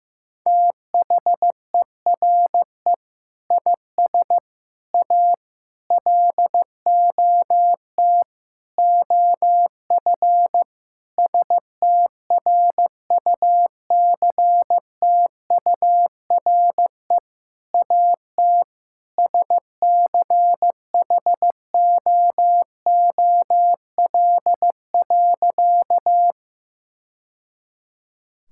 Morse code, heard as a keyed tone, about 15 wpm, 700 Hz. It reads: THERE IS A LOT OF STRUCTURE AT SCHOOL.
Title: Sentences from Top 1000 Words - Review Part 6 - 15wpm